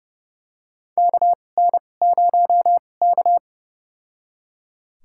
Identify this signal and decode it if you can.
Morse code, heard as a keyed tone, about 29 wpm, spaced as for 21 wpm, 690 Hz. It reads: KN0K
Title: USA Call Signs - 4 Characters - 2x Character Spacing - 30wpm